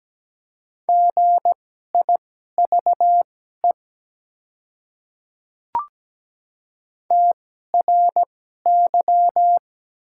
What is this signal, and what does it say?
Morse code, heard as a keyed tone, about 17 wpm, spaced as for 12 wpm, 700 Hz. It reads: GIVE TRY
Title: 2x Character Spacing - Top 300 Words - 17wpm